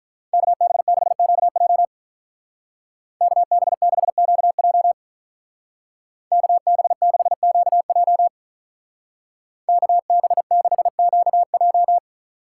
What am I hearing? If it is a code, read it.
Morse code, heard as a keyed tone, about 42 wpm, 700 Hz. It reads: KB6QJ KB6QJ KB6QJ KB6QJ
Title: Speed Racing - USA 2x2 Call Signs 35wpm